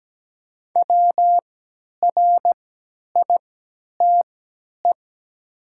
Morse code, heard as a keyed tone, about 17 wpm, spaced as for 10 wpm, 700 Hz. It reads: WRITE
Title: 3x Character Spacing - Top 300 Words - 17wpm